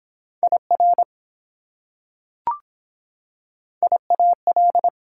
Morse code, heard as a keyed tone, about 23 wpm, 700 Hz. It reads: IR IAL